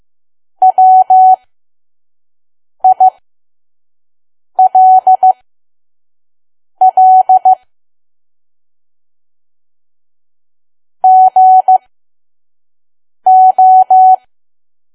Morse code, heard as a keyed tone, about 15 wpm, spaced as for 5 wpm, 750 Hz. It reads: WILL GO
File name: Archive / 5 WPM files / PM